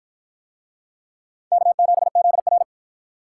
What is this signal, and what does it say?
Morse code, heard as a keyed tone, about 53 wpm, 700 Hz. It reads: K7ZR